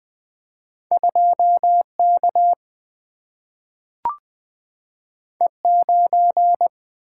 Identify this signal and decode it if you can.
Morse code, heard as a keyed tone, about 20 wpm, 700 Hz. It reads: 2K E9
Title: Single-Letter Single-Number 20wpm